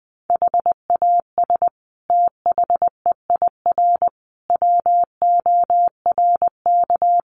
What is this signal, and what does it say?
Morse code, heard as a keyed tone, about 20 wpm, 700 Hz. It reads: HAS THEIR WORK
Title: Sentences from Top 1500 Words 20wpm